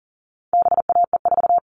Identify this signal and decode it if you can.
Morse code, heard as a keyed tone, about 40 wpm, 700 Hz. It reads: BAE4
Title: All Cut Numbers - 4-Digit 40wpm